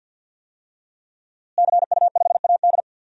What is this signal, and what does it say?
Morse code, heard as a keyed tone, about 52 wpm, 700 Hz. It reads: CU5AD
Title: DXCC-100 Call Signs - 5 Characters - Repeated 2x - 50wpm